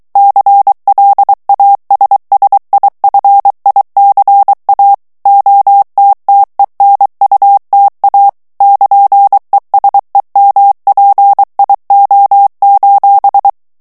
Morse code, heard as a keyed tone, about 23 wpm, 800 Hz. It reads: CLASSIFICA OTTENUTA <KN>ESEMPIO:
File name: ITALIEN texte clair 2 vitesse 22 mots minute